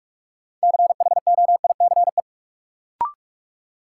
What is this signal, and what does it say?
Morse code, heard as a keyed tone, about 46 wpm, 700 Hz. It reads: CHOICE
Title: Sets of 5 Words - New Word Drill - 45wpm